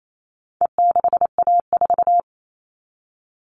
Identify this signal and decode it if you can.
Morse code, heard as a keyed tone, about 28 wpm, 700 Hz. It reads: E6A4
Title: All Cut Numbers - 4-Digit 28wpm